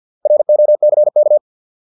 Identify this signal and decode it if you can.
Morse code, heard as a keyed tone, about 50 wpm, 600 Hz. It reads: ROCK